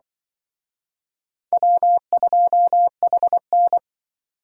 Morse code, heard as a keyed tone, about 23 wpm, 700 Hz. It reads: W2HN
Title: Speed Racing - USA Call Signs 20wpm